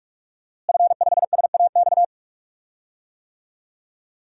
Morse code, heard as a keyed tone, about 43 wpm, 700 Hz. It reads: F5SAX
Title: DXCC-100 Call Signs - 5 Characters - Repeated 3x - 45wpm